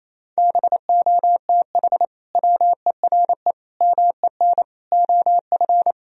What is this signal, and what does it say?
Morse code, heard as a keyed tone, about 28 wpm, 700 Hz. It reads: BOTH WERE MEN OF
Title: Sentences from Top 1000 Words - Review Part 3 - 28wpm